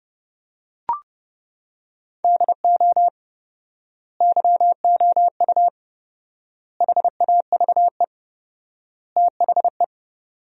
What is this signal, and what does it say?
Morse code, heard as a keyed tone, about 30 wpm, 700 Hz. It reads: DO YOU HAVE THE
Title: 4x Word-Spacing - Sentences from Top 300 Words - 30wpm